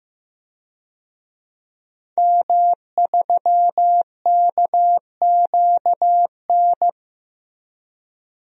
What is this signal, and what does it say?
Morse code, heard as a keyed tone, about 15 wpm, 700 Hz. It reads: M3KQN
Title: DXCC-100 Call Signs - 5 Characters - Repeated 3x - 15wpm